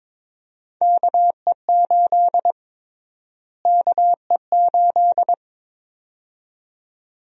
Morse code, heard as a keyed tone, about 22 wpm, 700 Hz. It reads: KE8 KE8
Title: USA Call Sign Prefixes - Repeated 2x - 22wpm